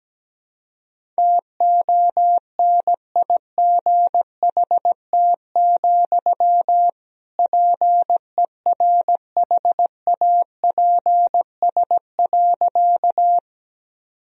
Morse code, heard as a keyed tone, about 17 wpm, 700 Hz. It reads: TONIGHT, PERHAPS.
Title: Sentences from Top 1000 Words - Review Part 3 - 17wpm